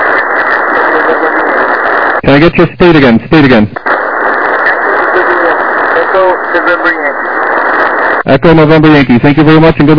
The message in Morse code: none